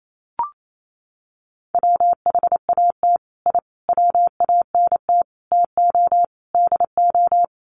Morse code, heard as a keyed tone, about 28 wpm, 700 Hz. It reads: WHAT I WANT TO DO
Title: Sets of 5 Words Encore 28wpm